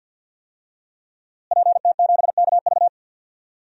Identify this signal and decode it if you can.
Morse code, heard as a keyed tone, about 52 wpm, 700 Hz. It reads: PT7KV
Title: DXCC-100 Call Signs - 5 Characters - Repeated 3x - 50wpm